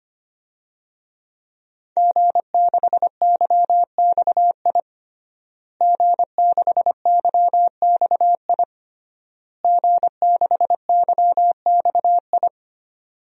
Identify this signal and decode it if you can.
Morse code, heard as a keyed tone, about 25 wpm, 700 Hz. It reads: G6YXI G6YXI G6YXI
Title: DXCC-100 Call Signs - 5 Characters - Repeated 3x - 25wpm